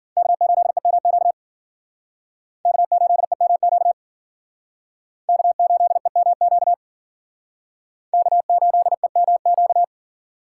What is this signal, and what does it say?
Morse code, heard as a keyed tone, about 51 wpm, 700 Hz. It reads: K8EMQ K8EMQ K8EMQ K8EMQ